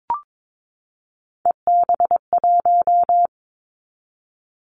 Morse code, heard as a keyed tone, about 22 wpm, 700 Hz. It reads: EB1
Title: Callsign Prefixes for Top 25 DXCC - 22wpm